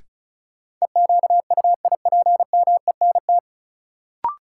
Morse code, heard as a keyed tone, about 36 wpm, 700 Hz. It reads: EQUIPMENT
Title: Sentences from Top 700 Words - New Word Drill - 35wpm